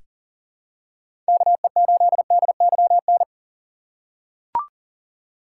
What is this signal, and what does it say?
Morse code, heard as a keyed tone, about 40 wpm, 700 Hz. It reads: KE8DYN
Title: USA Call Signs - 6 Characters - 6x Character Spacing - 40wpm